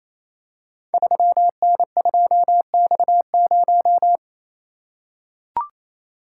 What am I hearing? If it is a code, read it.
Morse code, heard as a keyed tone, about 28 wpm, 700 Hz. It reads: 3N2X0
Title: Code Groups - 5 Letter:Numbers - 28wpm